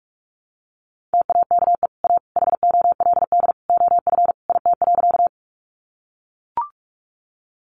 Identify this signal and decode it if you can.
Morse code, heard as a keyed tone, about 44 wpm, 700 Hz. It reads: TAKE A HOLD OF IT.
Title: Sentences from Top 400 Words 45wpm